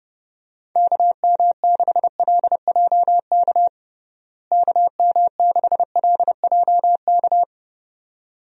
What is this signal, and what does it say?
Morse code, heard as a keyed tone, about 30 wpm, 700 Hz. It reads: KM6LJK KM6LJK
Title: USA Call Signs - 6 Characters - Repeated 2x - 30wpm